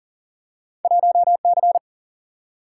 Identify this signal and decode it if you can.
Morse code, heard as a keyed tone, about 40 wpm, 700 Hz. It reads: 1 C
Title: Single-Letter Single-Number 40wpm